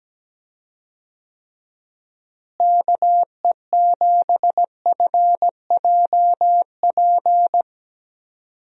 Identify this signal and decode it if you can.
Morse code, heard as a keyed tone, about 17 wpm, 700 Hz. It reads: KE7FJP
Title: USA Call Signs - 6 Characters - Repeated 3x - 17wpm